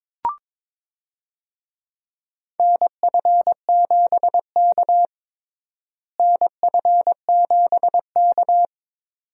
Morse code, heard as a keyed tone, about 22 wpm, 700 Hz. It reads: NF7K NF7K
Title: USA Call Signs - 4 Characters - Repeated 2x - 22wpm